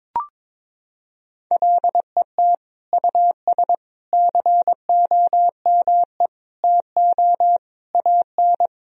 Morse code, heard as a keyed tone, about 22 wpm, 700 Hz. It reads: LET US COME TO AN
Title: Sentences from Top 1000 Words - Review Part 3 - 22wpm